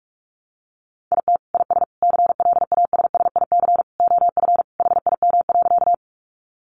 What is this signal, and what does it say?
Morse code, heard as a keyed tone, about 46 wpm, 700 Hz. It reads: IT IS CLASSIC OF HIM.
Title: Sentences from Top 1000 Words 45wpm